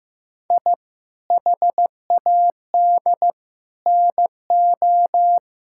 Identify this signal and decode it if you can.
Morse code, heard as a keyed tone, about 15 wpm, 700 Hz. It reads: I HAD NO
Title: Sentences from Top 1000 Words - Review Part 6 - 15wpm